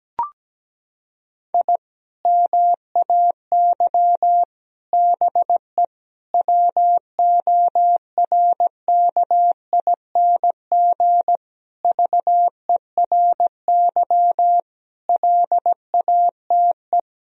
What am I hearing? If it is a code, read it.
Morse code, heard as a keyed tone, about 17 wpm, 700 Hz. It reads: I MAY BE WORKING VERY LATE
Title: Sentences from Top 1000 Words - Review Part 4 - 17wpm